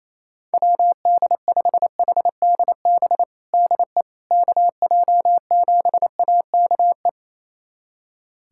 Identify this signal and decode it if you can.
Morse code, heard as a keyed tone, about 28 wpm, 700 Hz. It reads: WD5HDB DE KJ7AKE
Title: USA Callsign DE Callsign - 6 Characters - 28wpm